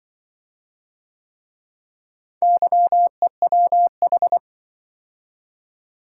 Morse code, heard as a keyed tone, about 25 wpm, 700 Hz. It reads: YEWH
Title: Speed-Racing - Four Letter-Number Combinations - 20wpm